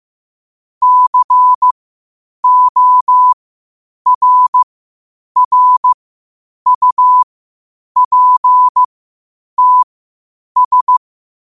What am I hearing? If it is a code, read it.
Morse code, heard as a keyed tone, about 15 wpm, spaced as for 8 wpm, 1000 Hz. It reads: CORRUPTS
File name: qotdfiles / 8wpm